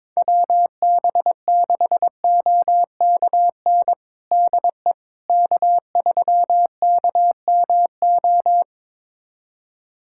Morse code, heard as a keyed tone, about 22 wpm, 700 Hz. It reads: WB6OKN DE K3KMO
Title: USA Callsign DE Callsign 22wpm